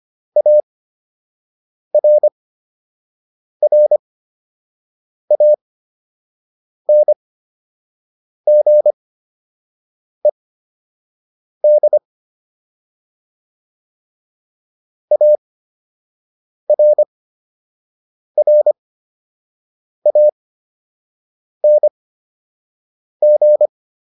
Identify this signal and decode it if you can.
Morse code, heard as a keyed tone, about 25 wpm, spaced as for 6 wpm, 600 Hz. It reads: ARRANGED ARRANG
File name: ED8 06